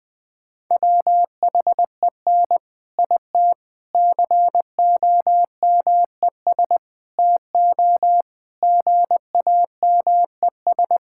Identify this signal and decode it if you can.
Morse code, heard as a keyed tone, about 21 wpm, 700 Hz. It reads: WHEN IT COMES TO GAMES
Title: Sentences with 3-Word Phrases - 20wpm